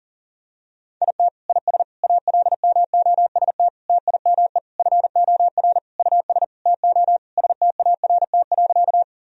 Morse code, heard as a keyed tone, about 40 wpm, 700 Hz. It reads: IT IS ALMOST TIME FOR US TO START.
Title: Sentences from Top 1000 Words - Review Part 4 - 40wpm